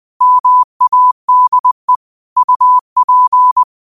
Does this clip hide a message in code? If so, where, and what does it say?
Yes: Morse code from 0.2 to 3.6 s, heard as a keyed tone, about 20 wpm, 1000 Hz. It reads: MADE UP